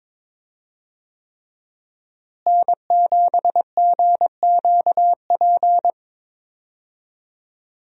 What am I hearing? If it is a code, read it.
Morse code, heard as a keyed tone, about 22 wpm, 700 Hz. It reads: N7GQP
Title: USA Call Signs - 5 Characters - Repeated 3x - 22wpm